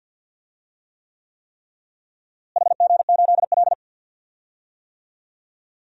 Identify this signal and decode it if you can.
Morse code, heard as a keyed tone, about 52 wpm, 700 Hz. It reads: HG8L
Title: DXCC-100 Call Signs - 4 Characters - Repeated 3x - 50wpm